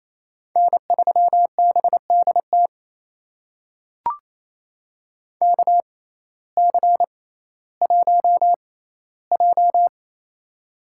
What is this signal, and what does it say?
Morse code, heard as a keyed tone, about 28 wpm, spaced as for 15 wpm, 700 Hz. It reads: N3BDT KC1J
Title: USA Call Signs - 6x Character Spacing - 28wpm